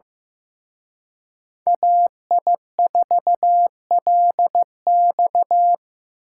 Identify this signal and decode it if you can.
Morse code, heard as a keyed tone, about 18 wpm, 700 Hz. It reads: AI4LX
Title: Speed Racing - USA 2x2 Call Signs 15wpm